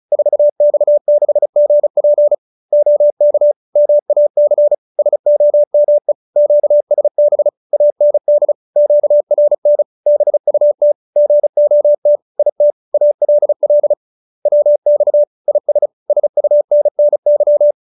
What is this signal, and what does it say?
Morse code, heard as a keyed tone, about 34 wpm, 590 Hz. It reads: 4X6GP OK MAC SOME QSB AND QRN BUT GOT IT ALL WX IS SUNNY
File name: QSO310 35